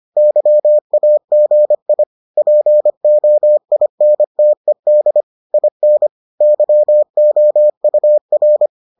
Morse code, heard as a keyed tone, about 25 wpm, 600 Hz. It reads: YAGI POINTED IN YOUR